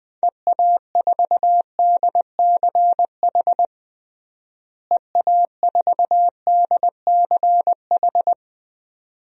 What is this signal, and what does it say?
Morse code, heard as a keyed tone, about 21 wpm, 700 Hz. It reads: EA4DCH EA4DCH
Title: DXCC-100 Call Signs - 6 Characters - Repeated 2x - 20wpm